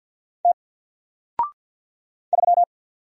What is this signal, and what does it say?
Morse code, heard as a keyed tone, about 48 wpm, 700 Hz. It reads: T 3
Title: Single Character (Letter:Number:Punctuation:Prosign) Alt 50wpm